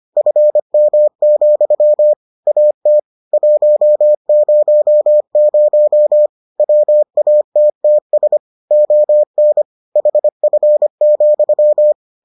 Morse code, heard as a keyed tone, about 25 wpm, 600 Hz. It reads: FM, AT 100 WATTS ON HF,